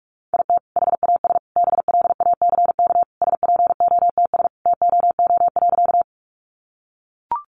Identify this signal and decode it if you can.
Morse code, heard as a keyed tone, about 44 wpm, 700 Hz. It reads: IT HAS BLACK SPOTS TOO.
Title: Additional Easy Phrases 45wpm